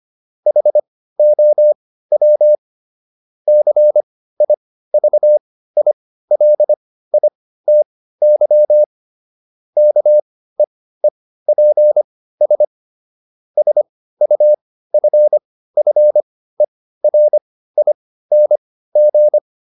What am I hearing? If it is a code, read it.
Morse code, heard as a keyed tone, about 25 wpm, spaced as for 15 wpm, 600 Hz. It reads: HOW CIVILITY KEEPS SUFFERING